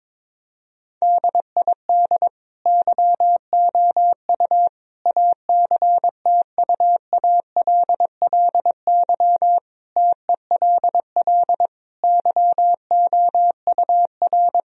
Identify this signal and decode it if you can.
Morse code, heard as a keyed tone, about 22 wpm, 700 Hz. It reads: DID YOU ACTUALLY TELL YOUR
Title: Sentences from Top 1000 Words - Review Part 4 - 22wpm